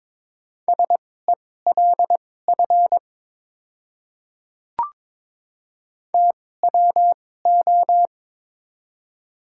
Morse code, heard as a keyed tone, about 22 wpm, spaced as for 16 wpm, 700 Hz. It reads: SELF TWO
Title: 2x Character Spacing - Top 300 Words - 22wpm